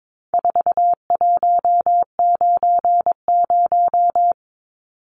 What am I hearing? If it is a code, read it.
Morse code, heard as a keyed tone, about 22 wpm, 700 Hz. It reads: 4190
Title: Numbers - 4-Digit 22wpm